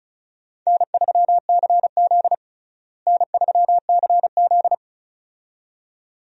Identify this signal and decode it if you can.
Morse code, heard as a keyed tone, about 36 wpm, 700 Hz. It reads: N3CZ N3CZ
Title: USA Call Signs - 4 Characters - Repeated 2x - 35wpm